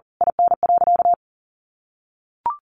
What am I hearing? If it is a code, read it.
Morse code, heard as a keyed tone, about 40 wpm, 700 Hz. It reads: IN.
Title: Easy Phrases 40wpm